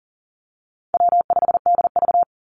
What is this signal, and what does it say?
Morse code, heard as a keyed tone, about 40 wpm, 700 Hz. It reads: W5DV
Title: Call Signs - 40wpm